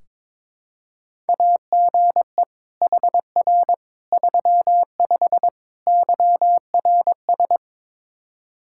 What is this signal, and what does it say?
Morse code, heard as a keyed tone, about 22 wpm, 700 Hz. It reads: AGE HR 35 YRS
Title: Practice QSO - Exchange 3 - More Info - 22wpm